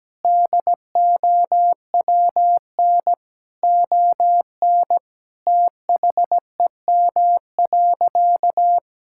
Morse code, heard as a keyed tone, about 17 wpm, 700 Hz. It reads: DOWN ON THEM.
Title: Sentences from Top 1600 Words 17wpm